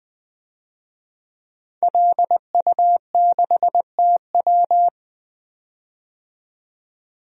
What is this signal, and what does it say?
Morse code, heard as a keyed tone, about 20 wpm, 700 Hz. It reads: LU6TW
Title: DXCC-100 Call Signs - 5 Characters - 20wpm